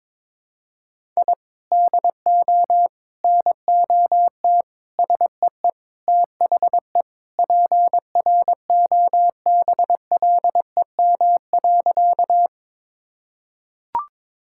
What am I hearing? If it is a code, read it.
Morse code, heard as a keyed tone, about 22 wpm, 700 Hz. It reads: I DO NOT SEE THE PROBLEM.
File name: practice / 22wpm / Sentences with 4-Word Phrases - 22wpm